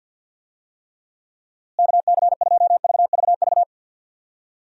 Morse code, heard as a keyed tone, about 51 wpm, 700 Hz. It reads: KC2VVV
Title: USA Call Signs - 6 Characters - Repeated 2x - 50wpm